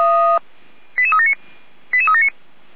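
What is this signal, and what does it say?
Five-tone selective call (ZVEI), heard as quick runs of five tones, tones of about 80 ms. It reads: 88379 88379
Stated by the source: ZVEI five-tone selective call received on an SDR, from a fire brigade's alerting recordings.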